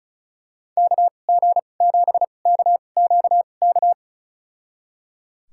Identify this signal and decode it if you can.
Morse code, heard as a keyed tone, about 36 wpm, spaced as for 25 wpm, 700 Hz. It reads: KG7KQK
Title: USA Call Signs - 6 Characters - 2x Character Spacing - 35wpm